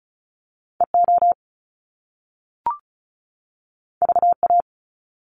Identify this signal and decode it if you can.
Morse code, heard as a keyed tone, about 34 wpm, 700 Hz. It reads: E O V A
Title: Top Two Letter Combinations in Words 35wpm